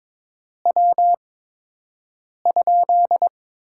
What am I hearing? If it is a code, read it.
Morse code, heard as a keyed tone, about 22 wpm, 700 Hz. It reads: W ?
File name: Top 90 QSO Elements - 8x Character Spacing - 22wpm